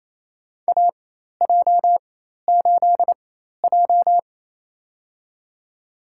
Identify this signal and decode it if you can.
Morse code, heard as a keyed tone, about 28 wpm, spaced as for 13 wpm, 700 Hz. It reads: AJ8J
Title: USA Call Signs - 4 Characters - 4x Character Spacing - 28wpm